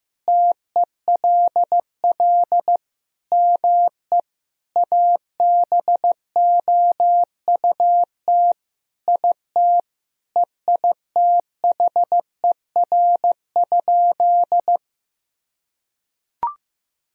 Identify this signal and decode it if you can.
Morse code, heard as a keyed tone, about 15 wpm, 700 Hz. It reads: TELL ME ABOUT IT EITHER?
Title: Sentences from Top 1000 Words - Review Part 4 - 15wpm